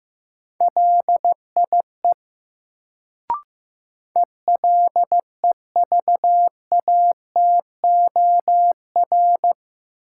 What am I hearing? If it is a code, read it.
Morse code, heard as a keyed tone, about 15 wpm, 700 Hz. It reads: LIE ELEVATOR
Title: Sentences from Top 1600 Words - New Word Drill - 15wpm